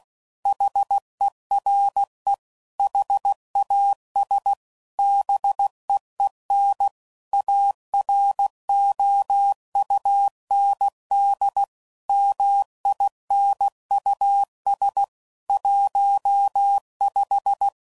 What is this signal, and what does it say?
Morse code, heard as a keyed tone, about 16 wpm, 800 Hz. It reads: HERE HAS BEEN AROUND MINUS 15
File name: CW Code Groups @15WPM / QSOs / WD0FBT